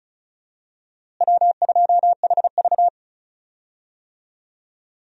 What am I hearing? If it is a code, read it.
Morse code, heard as a keyed tone, about 34 wpm, 700 Hz. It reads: W2HV
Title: USA Call Signs - 4 Characters - Repeated 3x - 35wpm